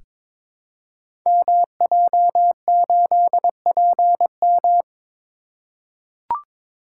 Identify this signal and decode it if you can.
Morse code, heard as a keyed tone, about 22 wpm, 700 Hz. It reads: MJ8PM
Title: Code Groups - 5 Letter:Numbers - 22wpm